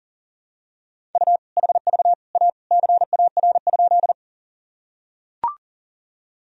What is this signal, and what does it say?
Morse code, heard as a keyed tone, about 37 wpm, 700 Hz. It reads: U HV A CAR?
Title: Sentences from Top 300 Words and Abbreviations 40wpm